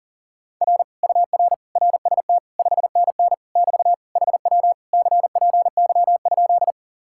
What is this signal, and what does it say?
Morse code, heard as a keyed tone, about 50 wpm, 700 Hz. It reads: R UR RST 5NN <BT> HW CPY?